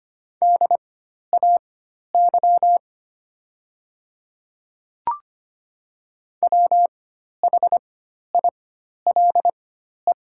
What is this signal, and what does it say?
Morse code, heard as a keyed tone, about 25 wpm, spaced as for 12 wpm, 700 Hz. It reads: DAY WHILE
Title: 4x Character Spacing - Top 300 Words - 25wpm